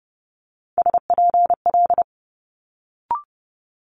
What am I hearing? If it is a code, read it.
Morse code, heard as a keyed tone, about 30 wpm, 700 Hz. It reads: SPL
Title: Multi-letter Phonemes Quick Fire 30wpm